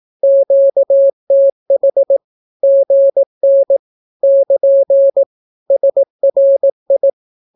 Morse code, heard as a keyed tone, about 18 wpm, 550 Hz. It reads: QTH GN <KN> SRI
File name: naukapodstaw / skroty / skroty